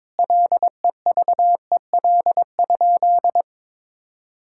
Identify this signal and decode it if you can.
Morse code, heard as a keyed tone, about 21 wpm, 700 Hz. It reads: LEVEL?
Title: Sentences from Top 1000 Words - Review Part 5 - 22wpm